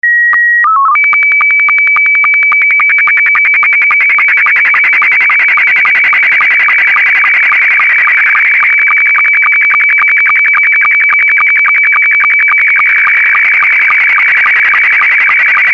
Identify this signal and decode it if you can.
SSTV (slow-scan television), Scottie 2.